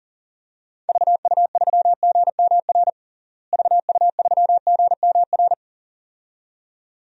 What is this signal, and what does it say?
Morse code, heard as a keyed tone, about 40 wpm, 700 Hz. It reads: VU3GMR VU3GMR
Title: DXCC-100 Call Signs - 6 Characters - Repeated 2x - 40wpm